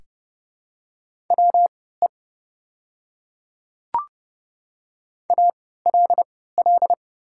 Morse code, heard as a keyed tone, about 30 wpm, spaced as for 17 wpm, 700 Hz. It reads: WE ALL
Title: 3x Character Spacing - Top 300 Words - 30wpm